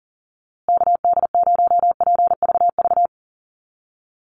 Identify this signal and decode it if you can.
Morse code, heard as a keyed tone, about 40 wpm, 700 Hz. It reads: KD0PVV
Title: USA Call Signs Repeated 3x 40wpm